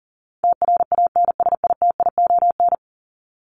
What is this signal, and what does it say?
Morse code, heard as a keyed tone, about 40 wpm, 700 Hz. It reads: TRANSITION
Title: Top 1751-2000 Words 40wpm